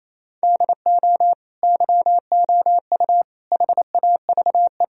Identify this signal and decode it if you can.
Morse code, heard as a keyed tone, about 28 wpm, 700 Hz. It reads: DO YOU HAVE
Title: Sentences from Top 1000 Words - Review Part 3 - 28wpm